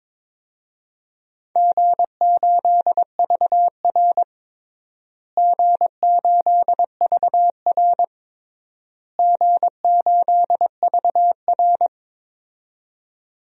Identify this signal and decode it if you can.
Morse code, heard as a keyed tone, about 21 wpm, 700 Hz. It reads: G8VR G8VR G8VR
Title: DXCC-100 Call Signs - 4 Characters - Repeated 3x - 22wpm